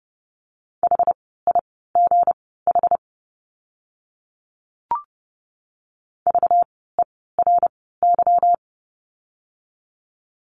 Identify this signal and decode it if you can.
Morse code, heard as a keyed tone, about 30 wpm, spaced as for 17 wpm, 700 Hz. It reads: HIGH VERY
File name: course / 30:10wpm / 237 Lesson 38f - Letter q - Words 30wpm